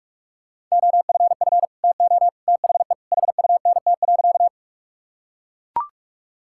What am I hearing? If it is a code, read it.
Morse code, heard as a keyed tone, about 43 wpm, 700 Hz. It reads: OFF TO THE HUNT.
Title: Sentences from Top 1600 Words 45wpm